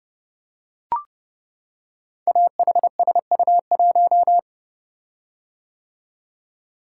Morse code, heard as a keyed tone, about 28 wpm, 700 Hz. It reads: AHSU1
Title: Code Groups - 5 Letter:Numbers - 30wpm